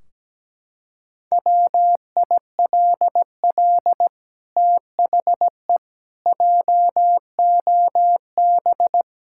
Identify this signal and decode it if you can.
Morse code, heard as a keyed tone, about 17 wpm, 700 Hz. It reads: WILL THE JOB
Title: Sentences from Top 1600 Words 17wpm